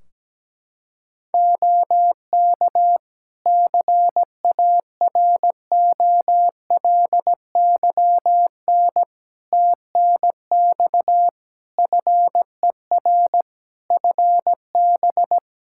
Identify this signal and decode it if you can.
Morse code, heard as a keyed tone, about 17 wpm, 700 Hz. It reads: OK CAROLYN TNX FER FB
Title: Practice QSO - Exchange 4 - Ending - 17wpm